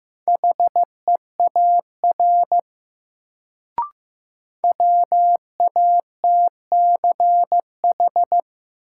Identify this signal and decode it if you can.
Morse code, heard as a keyed tone, about 15 wpm, 700 Hz. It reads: HEAR WATCH
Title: Sentences from Top 300 Words - New Word Drill - 15wpm